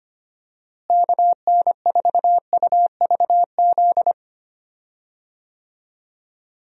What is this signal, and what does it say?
Morse code, heard as a keyed tone, about 25 wpm, 700 Hz. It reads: KN4UVZ
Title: USA Call Signs - 6 Characters - Repeated 2x - 25wpm